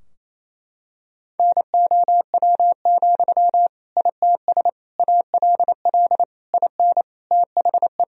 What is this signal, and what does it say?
Morse code, heard as a keyed tone, about 28 wpm, 700 Hz. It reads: NOW, ITS ALL IN THE
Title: Sentences from Top 1000 Words - Review Part 3 - 28wpm